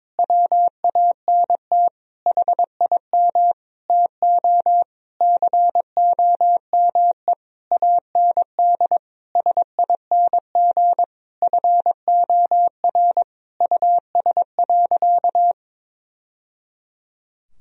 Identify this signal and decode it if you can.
Morse code, heard as a keyed tone, about 22 wpm, 700 Hz. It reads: WANT HIM TO COME AND SING FOR US.
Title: Sentences from Top 1000 Words - Review Part 2 - 22wpm